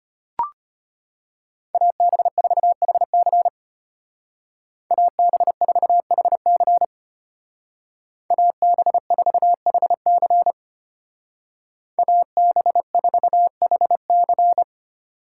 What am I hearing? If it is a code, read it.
Morse code, heard as a keyed tone, about 31 wpm, 700 Hz. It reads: AB4HC AB4HC AB4HC AB4HC
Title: Speed Racing - USA 2x2 Call Signs 25wpm